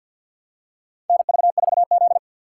Morse code, heard as a keyed tone, about 53 wpm, 690 Hz. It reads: NV4Z
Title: USA Call Signs - 4 Characters - 6x Character Spacing - 50wpm